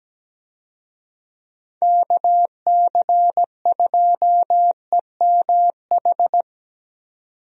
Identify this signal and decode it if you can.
Morse code, heard as a keyed tone, about 17 wpm, 700 Hz. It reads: KC2EMH
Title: USA Call Signs - 6 Characters - 17wpm